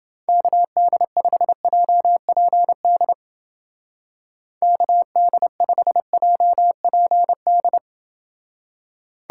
Morse code, heard as a keyed tone, about 31 wpm, 700 Hz. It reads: KD5JPD KD5JPD